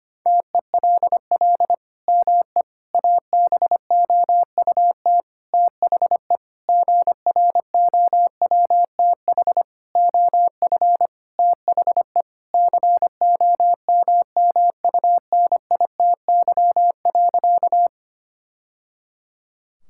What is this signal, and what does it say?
Morse code, heard as a keyed tone, about 25 wpm, 700 Hz. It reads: TELL ME ABOUT THE GROWTH OF THE COMMUNITY.
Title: Sentences from Top 1000 Words - Review Part 5 - 25wpm